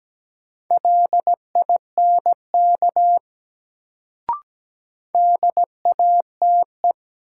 Morse code, heard as a keyed tone, about 17 wpm, 700 Hz. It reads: LINK DATE
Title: Sentences from Top 1000 Words - Review Part 7 - New Word Drill - 17wpm